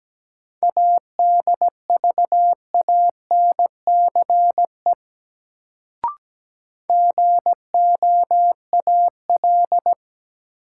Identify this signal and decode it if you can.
Morse code, heard as a keyed tone, about 17 wpm, 700 Hz. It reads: ADVANCE GOAL
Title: Sentences from Top 1400 Words - New Word Drill Alt - 17wpm